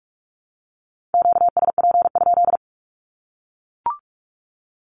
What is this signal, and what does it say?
Morse code, heard as a keyed tone, about 44 wpm, 700 Hz. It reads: QSP?
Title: Q-Codes Repeated 3x 45wpm